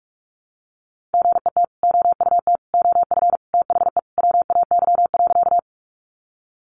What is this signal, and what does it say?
Morse code, heard as a keyed tone, about 44 wpm, 700 Hz. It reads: GET OUT OF THE WAY.
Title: Sentences from Top 200 Words 45wpm